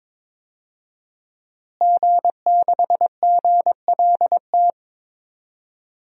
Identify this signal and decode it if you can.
Morse code, heard as a keyed tone, about 22 wpm, 700 Hz. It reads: G6GLT
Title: DXCC-100 Call Signs - 5 Characters - Repeated 2x - 22wpm